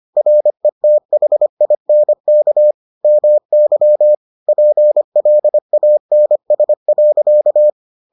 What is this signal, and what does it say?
Morse code, heard as a keyed tone, about 25 wpm, 600 Hz. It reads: RETHINK MY PLANS.